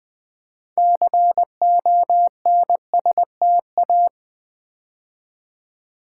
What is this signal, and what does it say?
Morse code, heard as a keyed tone, about 20 wpm, 700 Hz. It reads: CONSTA
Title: Build A Word - 3 Letters At A Time - 8-Letter Words - 20wpm